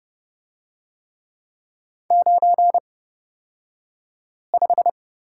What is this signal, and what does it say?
Morse code, heard as a keyed tone, about 30 wpm, spaced as for 5 wpm, 700 Hz. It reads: 95
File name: Lesson 37d - Letter x - Two Characters 30wpm